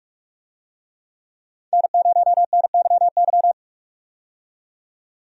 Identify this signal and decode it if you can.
Morse code, heard as a keyed tone, about 46 wpm, 700 Hz. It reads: N0NYY
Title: USA Call Signs - 5 Characters - 45wpm